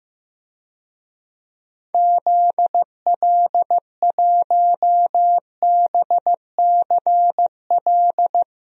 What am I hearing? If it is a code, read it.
Morse code, heard as a keyed tone, about 15 wpm, 700 Hz. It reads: ZL1BCL
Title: DXCC-100 Call Signs - 6 Characters - 15wpm